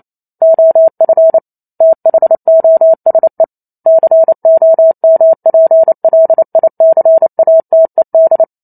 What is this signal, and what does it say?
Morse code, heard as a keyed tone, about 28 wpm, 650 Hz. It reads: OF THOSE COMPLICATED